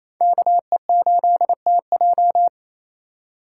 Morse code, heard as a keyed tone, about 28 wpm, 700 Hz. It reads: KE8TJ
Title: USA Call Signs - 5 Characters - 4x Character Spacing - 28wpm